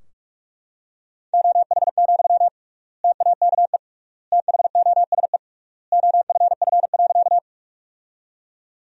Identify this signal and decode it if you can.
Morse code, heard as a keyed tone, about 46 wpm, 700 Hz. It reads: OH, TAKE THOSE OFF.
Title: 3x Word-Spacing - Sentences from Top 500 Words - 45wpm